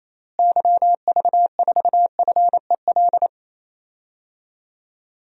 Morse code, heard as a keyed tone, about 28 wpm, 700 Hz. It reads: YV4FEL
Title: DXCC-100 Call Signs - 6 Characters - Repeated 3x - 28wpm